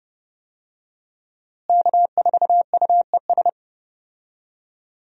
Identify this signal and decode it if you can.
Morse code, heard as a keyed tone, about 30 wpm, 700 Hz. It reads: K4UES
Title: USA Call Signs - 5 Characters - Repeated 3x - 30wpm